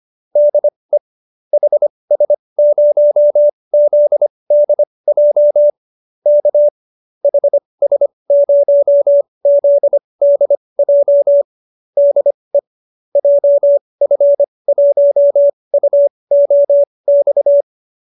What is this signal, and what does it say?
Morse code, heard as a keyed tone, about 25 wpm, spaced as for 20 wpm, 570 Hz. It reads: DE HS0ZDJ K HS0ZDJ DE JF1UOX
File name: qso216b 20